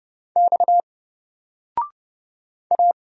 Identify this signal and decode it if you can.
Morse code, heard as a keyed tone, about 30 wpm, 700 Hz. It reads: X A
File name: Single-Letter-Number-30wpm